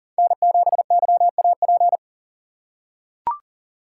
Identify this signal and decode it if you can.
Morse code, heard as a keyed tone, about 40 wpm, 700 Hz. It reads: N7YAP
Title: USA Call Signs - 6x Character Spacing - 40wpm